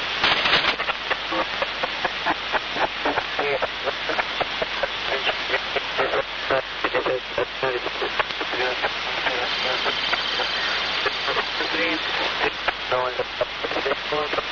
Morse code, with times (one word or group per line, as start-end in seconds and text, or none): none